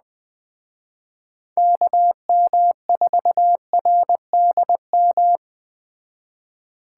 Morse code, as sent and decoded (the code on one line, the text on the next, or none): -.- -- ....- .-. -.. --
KM4RDM